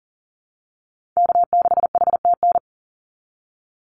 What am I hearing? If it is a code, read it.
Morse code, heard as a keyed tone, about 40 wpm, 700 Hz. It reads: K6HTN
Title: Call Signs Repeated 3x - 40wpm